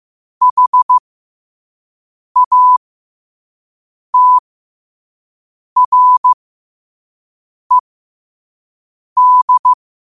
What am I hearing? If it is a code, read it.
Morse code, heard as a keyed tone, about 15 wpm, spaced as for 5 wpm, 1000 Hz. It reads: HATRED
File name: qotdfiles / 5wpm